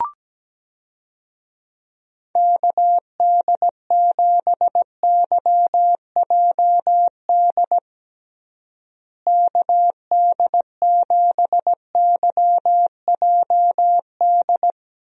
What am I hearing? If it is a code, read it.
Morse code, heard as a keyed tone, about 17 wpm, 700 Hz. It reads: KD7YJD KD7YJD
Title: USA Call Signs - 6 Characters - Repeated 2x - 17wpm